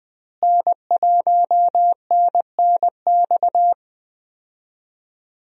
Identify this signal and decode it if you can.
Morse code, heard as a keyed tone, about 20 wpm, 700 Hz. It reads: N1NNX
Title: USA Call Signs - 5 Characters - Repeated 3x - 20wpm